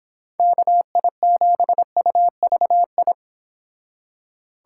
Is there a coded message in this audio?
Yes: Morse code from 0.4 to 3.1 s, heard as a keyed tone, about 28 wpm, 700 Hz. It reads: KI7UVI